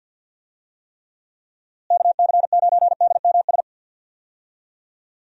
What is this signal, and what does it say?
Morse code, heard as a keyed tone, about 47 wpm, 700 Hz. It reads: KC9DMS
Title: USA Call Signs - 6 Characters - Repeated 3x - 50wpm